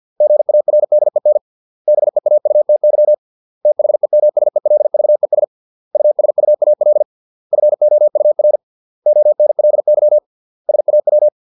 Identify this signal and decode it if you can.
Morse code, heard as a keyed tone, about 50 wpm, 600 Hz. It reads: GARDEN BEAUTY THEMSELVES USUAL FOUR ONLY SAW
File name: 632 words / 632 Words 50WPM